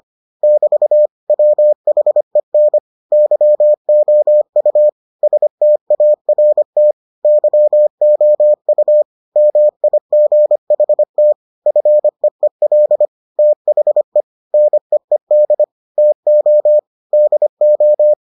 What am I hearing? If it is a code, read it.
Morse code, heard as a keyed tone, about 25 wpm, 600 Hz. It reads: <BT> WHEN YOU START YOU MIGHT FEEL THE NEED TO DO